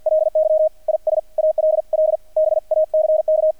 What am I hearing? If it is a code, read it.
Morse code, heard as a keyed tone, about 48 wpm, 620 Hz, with radio noise. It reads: LY IS ALL BACK
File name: archived / CRSnet